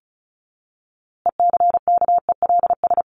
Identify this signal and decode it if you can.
Morse code, heard as a keyed tone, about 34 wpm, 700 Hz. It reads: ECKELS
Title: A Sound Of Thunder 35wpm